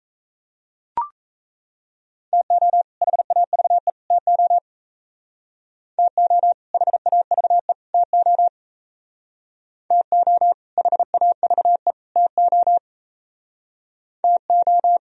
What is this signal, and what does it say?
Morse code, heard as a keyed tone, about 35 wpm, 700 Hz. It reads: TO HAVE TO TO HAVE TO TO HAVE TO TO